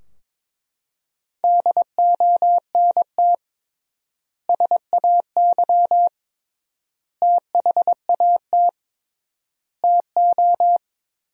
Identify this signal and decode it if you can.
Morse code, heard as a keyed tone, about 22 wpm, 700 Hz. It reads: DONT SAY THAT TO
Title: 3x Word-Spacing - Sentences from Top 300 Words - 22wpm